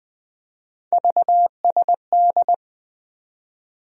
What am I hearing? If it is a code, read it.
Morse code, heard as a keyed tone, about 21 wpm, 700 Hz. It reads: VSD
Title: USA Call Sign Postfixes - 3 Letters 20wpm